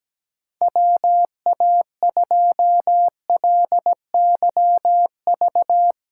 Morse code, heard as a keyed tone, about 17 wpm, 700 Hz. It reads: WA2LYV
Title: USA Call Signs - 6 Characters - 3x Character Spacing - 17wpm